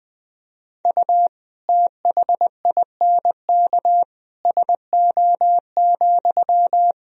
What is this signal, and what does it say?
Morse code, heard as a keyed tone, about 20 wpm, 700 Hz. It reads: U THINK SO,
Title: Sentences from Top 200 Words and Abbreviations 20wpm